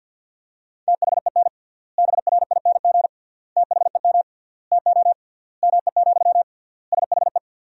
Morse code, heard as a keyed tone, about 50 wpm, 700 Hz. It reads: THEN BRING THEM TO ME, SHE
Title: 3x Word-Spacing - Sentences from Top 500 Words - 50wpm